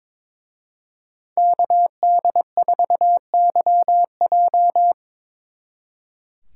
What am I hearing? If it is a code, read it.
Morse code, heard as a keyed tone, about 26 wpm, 700 Hz. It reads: KD4YJ
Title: Speed Racing - USA 2x2 Call Signs 22wpm